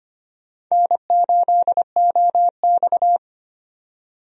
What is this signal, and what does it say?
Morse code, heard as a keyed tone, about 25 wpm, 700 Hz. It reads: N8OX
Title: USA Call Signs - 4 Characters - 25wpm